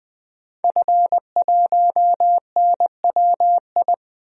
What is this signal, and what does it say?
Morse code, heard as a keyed tone, about 20 wpm, 700 Hz. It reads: F1NWI
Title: DXCC-100 Call Signs - 5 Characters - 20wpm